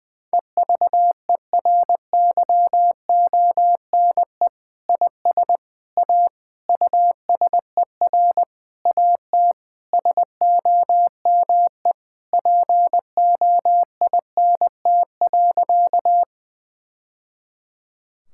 Morse code, heard as a keyed tone, about 20 wpm, 700 Hz. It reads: EVERYONE IS A USER AT SOME POINT.
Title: Sentences from Top 1000 Words - Review Part 5 - 20wpm